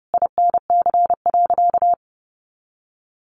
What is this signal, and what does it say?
Morse code, heard as a keyed tone, about 28 wpm, 700 Hz. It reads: INC.
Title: A Sound Of Thunder 30wpm